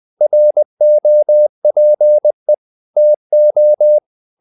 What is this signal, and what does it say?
Morse code, heard as a keyed tone, about 20 wpm, 600 Hz. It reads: ROPE TO
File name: the old man and the sea part 2 20wpm